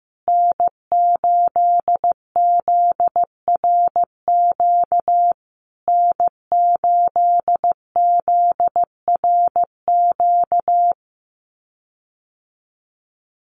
Morse code, heard as a keyed tone, about 15 wpm, 700 Hz. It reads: N8ZRQ N8ZRQ
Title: USA Call Signs Repeated 3x 15wpm